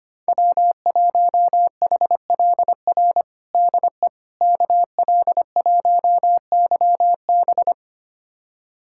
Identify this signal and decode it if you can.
Morse code, heard as a keyed tone, about 25 wpm, 700 Hz. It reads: W1HLR DE KL1YB
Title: USA Callsign DE Callsign - 5 Characters - 25wpm